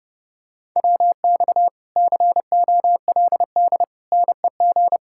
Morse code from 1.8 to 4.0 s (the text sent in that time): COLD